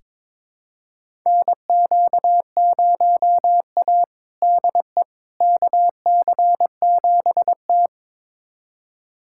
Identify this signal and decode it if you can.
Morse code, heard as a keyed tone, about 23 wpm, 700 Hz. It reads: NQ0A DE KC7T
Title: USA Callsign DE Callsign - 4 Characters - 22wpm